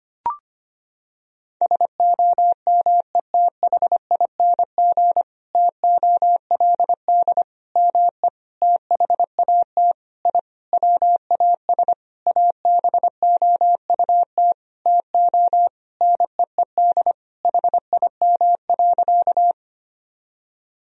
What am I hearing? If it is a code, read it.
Morse code, heard as a keyed tone, about 25 wpm, 700 Hz. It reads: SOMETHING TOLD ME THAT I WAS ABOUT TO NEED HIM.
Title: Sentences from Top 1000 Words - Review Part 1 - 25wpm